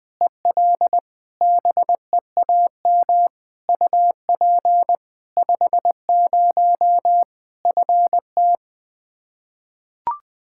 Morse code, heard as a keyed tone, about 20 wpm, 700 Hz. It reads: EL BEAM UP 50 FT
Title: K1USN SST QSO Practice 20wpm